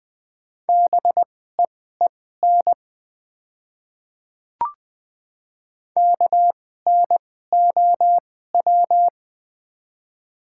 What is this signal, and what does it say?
Morse code, heard as a keyed tone, about 20 wpm, spaced as for 15 wpm, 700 Hz. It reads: BEEN KNOW
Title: 2x Character Spacing - Top 100 Words - 20wpm